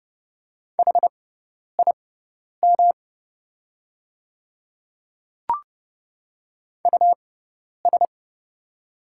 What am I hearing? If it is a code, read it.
Morse code, heard as a keyed tone, about 30 wpm, spaced as for 10 wpm, 690 Hz. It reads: HIM US
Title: 6x Character Spacing - Top 500 Words - 30wpm